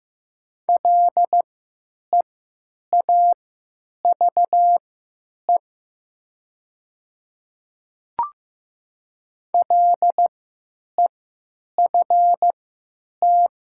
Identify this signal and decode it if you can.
Morse code, heard as a keyed tone, about 15 wpm, spaced as for 9 wpm, 700 Hz. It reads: LEAVE LEFT
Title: 3x Character Spacing - Top 500 Words - 15wpm